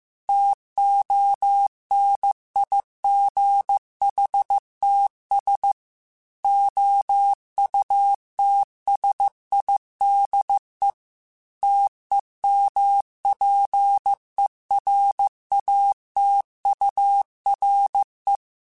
Morse code, heard as a keyed tone, about 15 wpm, 800 Hz. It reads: TONIGHTS OUTSIDE TEMPERATURE
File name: Weather Sets / Weather Sets-mp3s / Weather 3 17 @13